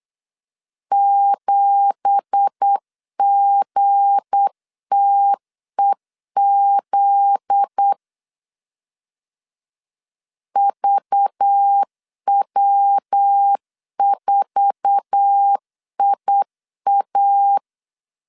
Morse code, heard as a keyed tone, about 8 wpm, 800 Hz. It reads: 7GTEZ VW4IA
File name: 8wpm